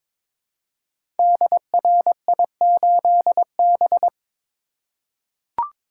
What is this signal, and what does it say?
Morse code, heard as a keyed tone, about 22 wpm, 700 Hz. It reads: DRI8B
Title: Code Groups - 5 Letter:Numbers - 22wpm